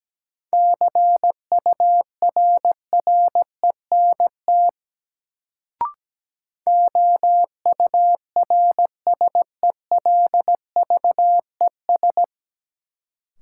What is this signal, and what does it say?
Morse code, heard as a keyed tone, about 17 wpm, 700 Hz. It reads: CURRENT OURSELVES
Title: Sentences from Top 1600 Words - New Word Drill Alt - 17wpm